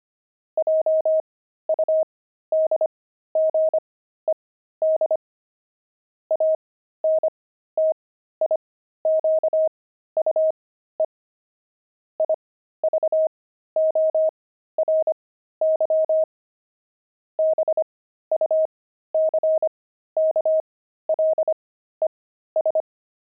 Morse code, heard as a keyed tone, about 25 wpm, spaced as for 13 wpm, 630 Hz. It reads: JUDGED ANTIQUE IVORY BUCKLES
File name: pangram-fs-13-cs-25-buckles